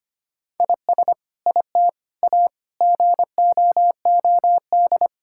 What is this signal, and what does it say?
Morse code, heard as a keyed tone, about 25 wpm, 700 Hz. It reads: IS IT A GOOD
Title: Sentences from Top 1000 Words - Review Part 6 - 25wpm